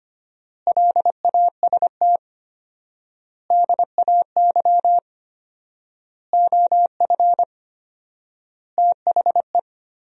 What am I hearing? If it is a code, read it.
Morse code, heard as a keyed tone, about 25 wpm, 700 Hz. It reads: LAST DAY OF THE
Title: 4x Word-Spacing - Sentences from Top 300 Words - 25wpm